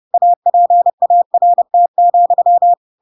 Morse code, heard as a keyed tone, about 30 wpm, 700 Hz. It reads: APART,
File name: Pinocchio / Pinocchio ch3 30 30 ik1hge